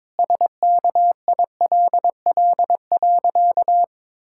Morse code, heard as a keyed tone, about 22 wpm, 700 Hz. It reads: SKILL.
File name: practice / 22wpm / Sentences with 3-Word Phrases - 22wpm